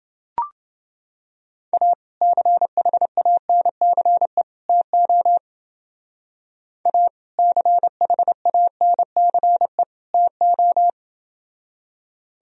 Morse code, heard as a keyed tone, about 28 wpm, 700 Hz. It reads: A CHANCE TO A CHANCE TO